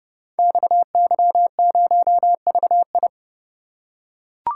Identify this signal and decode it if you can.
Morse code, heard as a keyed tone, about 30 wpm, 700 Hz. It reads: XY0VI
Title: Code Groups - 5 Letter:Numbers - 30wpm